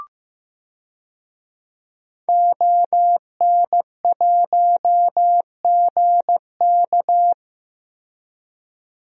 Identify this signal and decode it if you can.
Morse code, heard as a keyed tone, about 15 wpm, 700 Hz. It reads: ON1GK
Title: DXCC-100 Call Signs - 5 Characters - Repeated 2x - 15wpm